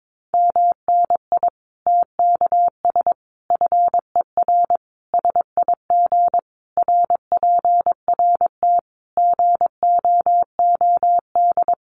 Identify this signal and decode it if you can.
Morse code, heard as a keyed tone, about 22 wpm, 700 Hz. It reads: MNI TKS FER SIG RPRT GOOD
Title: Common QSO Phrases 22wpm